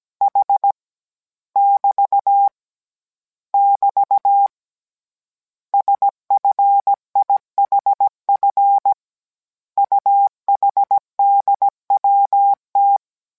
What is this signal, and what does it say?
Morse code, heard as a keyed tone, about 17 wpm, 800 Hz. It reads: H <BT> <BT> SFIHF UHDWT